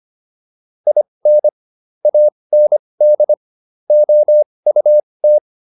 Morse code, heard as a keyed tone, about 25 wpm, spaced as for 20 wpm, 600 Hz. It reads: IN AND OUT
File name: WD304 20